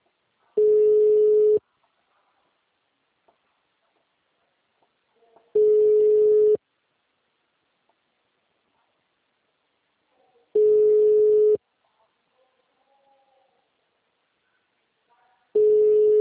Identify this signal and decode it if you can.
Ringback tone (Europe, 425 Hz).
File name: out- -2006- - - .44